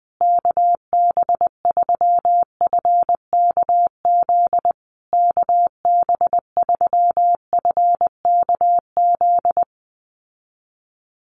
Morse code, heard as a keyed tone, about 20 wpm, 700 Hz. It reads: KB3FKZ KB3FKZ
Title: USA Call Signs Repeated 3x 20wpm